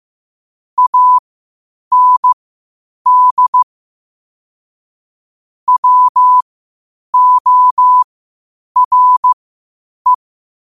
Morse code, heard as a keyed tone, about 15 wpm, spaced as for 8 wpm, 1000 Hz. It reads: AND WORE
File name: qotdfiles / 8wpm